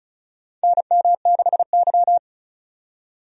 Morse code, heard as a keyed tone, about 36 wpm, 700 Hz. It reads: NM6Y
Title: USA Call Signs - 4 Characters - 35wpm